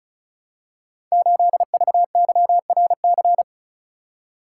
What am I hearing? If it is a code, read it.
Morse code, heard as a keyed tone, about 36 wpm, 700 Hz. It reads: 8VYRC
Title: Code Groups - 5 Letter:Numbers - 35wpm